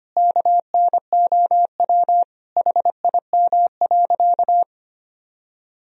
Morse code, heard as a keyed tone, about 25 wpm, 700 Hz. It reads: KNOW HIM.
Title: Sentences from Top 1000 Words - Review Part 3 - 25wpm